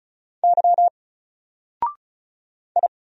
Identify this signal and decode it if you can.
Morse code, heard as a keyed tone, about 34 wpm, 700 Hz. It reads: Y I
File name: Single-Letter-Number-35wpm